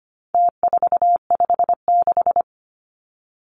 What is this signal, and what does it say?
Morse code, heard as a keyed tone, about 25 wpm, 700 Hz. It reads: T456
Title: Numbers - Contest Style - 4-Digit 25wpm